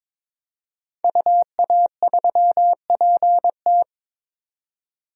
Morse code, heard as a keyed tone, about 21 wpm, 700 Hz. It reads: UA3PT
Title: DXCC-100 Call Signs - 5 Characters - 22wpm